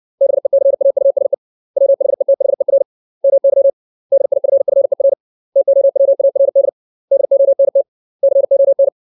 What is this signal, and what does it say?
Morse code, heard as a keyed tone, about 60 wpm, 550 Hz. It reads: BECAUSE WHETHER MY DIFFER TOWARD DONT CON
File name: 632 words / 632 Words 60WPM